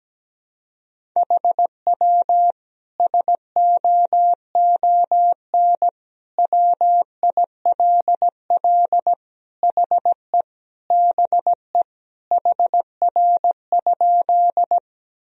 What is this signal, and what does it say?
Morse code, heard as a keyed tone, about 17 wpm, 700 Hz. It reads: HW SOON WILL HE BE HR?
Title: Sentences from Top 400 Words and Abbreviations 17wpm